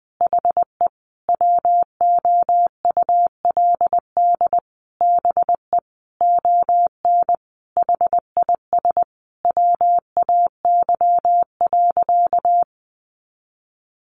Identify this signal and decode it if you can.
Morse code, heard as a keyed tone, about 20 wpm, 700 Hz. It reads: HE WOULD BE ON HIS WAY.
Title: Sentences from Top 300 Words 20wpm